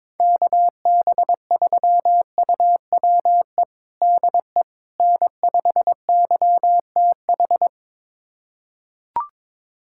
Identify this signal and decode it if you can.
Morse code, heard as a keyed tone, about 22 wpm, 700 Hz. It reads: KB3UWE DE N5YTH
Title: USA Callsign DE Callsign 22wpm